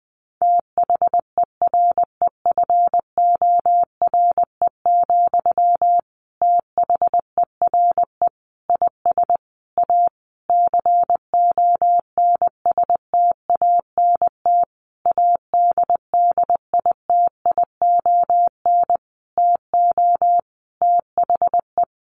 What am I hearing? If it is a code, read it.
Morse code, heard as a keyed tone, about 20 wpm, 700 Hz. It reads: THEREFORE, THERE IS A CONSTANT ADDITION TO THE
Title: Sentences from Top 1500 Words 20wpm